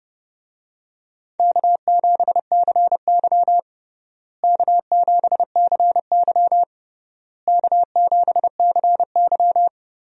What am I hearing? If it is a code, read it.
Morse code, heard as a keyed tone, about 30 wpm, 700 Hz. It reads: K7CY K7CY K7CY
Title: USA Call Signs - 4 Characters - Repeated 3x - 30wpm